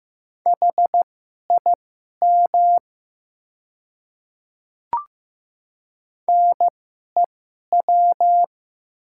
Morse code, heard as a keyed tone, about 16 wpm, spaced as for 11 wpm, 700 Hz. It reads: HIM NEW
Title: 2x Character Spacing - Top 200 Words - 15wpm